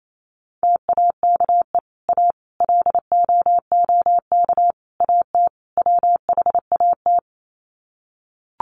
Morse code, heard as a keyed tone, about 28 wpm, 700 Hz. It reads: TAKE A LOOK AT WHAT
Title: Sets of 5 Words Encore2 28wpm